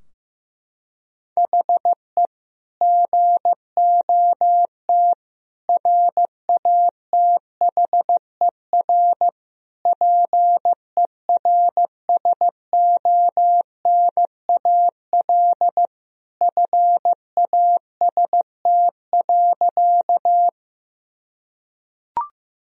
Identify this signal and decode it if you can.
Morse code, heard as a keyed tone, about 15 wpm, 700 Hz. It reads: HE GOT RATHER PERSONAL FAST.
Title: Sentences from Top 1000 Words - Review Part 4 - 15wpm